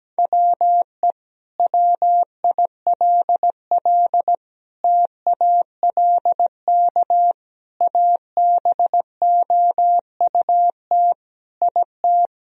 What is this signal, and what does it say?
Morse code, heard as a keyed tone, about 17 wpm, 700 Hz. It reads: WE WILL TALK ABOUT IT
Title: Sentences from Top 1000 Words - Review Part 4 - 17wpm